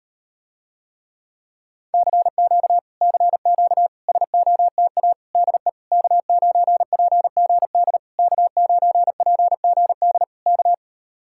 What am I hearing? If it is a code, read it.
Morse code, heard as a keyed tone, about 39 wpm, 700 Hz. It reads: CQ CQ SOTA DE K9PGD K9PGD K